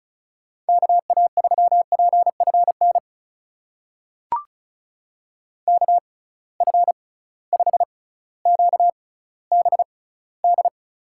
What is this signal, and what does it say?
Morse code, heard as a keyed tone, about 36 wpm, spaced as for 18 wpm, 700 Hz. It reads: KA3PFN KF5QBD
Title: USA Call Signs - 6 Characters - 6x Character Spacing - 35wpm